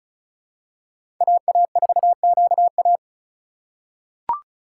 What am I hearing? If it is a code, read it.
Morse code, heard as a keyed tone, about 36 wpm, 700 Hz. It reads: AA4QA
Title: USA Call Signs - 5 Characters - 2x Character Spacing - 35wpm